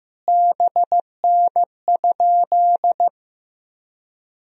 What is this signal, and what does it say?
Morse code, heard as a keyed tone, about 15 wpm, 700 Hz. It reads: BN?
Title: Sentences from Top 400 Words and Abbreviations 15wpm